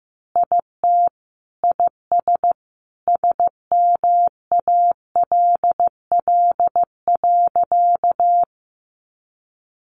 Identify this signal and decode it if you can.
Morse code, heard as a keyed tone, about 15 wpm, 700 Hz. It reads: IT IS SMALL.
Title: Additional Easy Phrases 15wpm